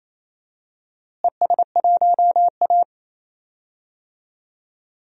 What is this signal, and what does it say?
Morse code, heard as a keyed tone, about 27 wpm, 700 Hz. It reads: ES1A
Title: DXCC-100 Call Signs - 4 Characters - Repeated 3x - 28wpm